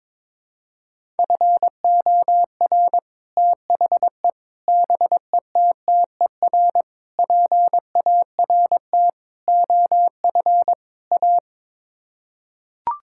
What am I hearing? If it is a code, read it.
Morse code, heard as a keyed tone, about 22 wpm, 700 Hz. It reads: FOR THE BETTER PART OF A
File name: practice / 22wpm / Sets of 6 Words - 22wpm